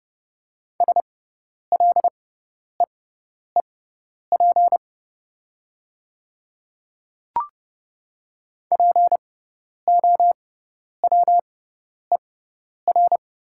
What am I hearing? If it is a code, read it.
Morse code, heard as a keyed tone, about 30 wpm, spaced as for 10 wpm, 690 Hz. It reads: SLEEP POWER
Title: 6x Character Spacing - Top 500 Words - 30wpm